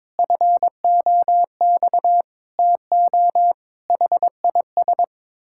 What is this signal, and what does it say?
Morse code, heard as a keyed tone, about 22 wpm, 700 Hz. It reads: FOX TO HIS
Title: Pangram Challenge v2 - 22wpm